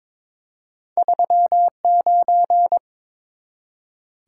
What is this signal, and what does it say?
Morse code, heard as a keyed tone, about 22 wpm, 690 Hz. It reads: 39
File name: practice / 22wpm / Two Letter-Number Combinations - 22wpm